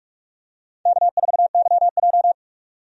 Morse code, heard as a keyed tone, about 46 wpm, 700 Hz. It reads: K4YJ
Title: USA Call Signs - 4 Characters - 45wpm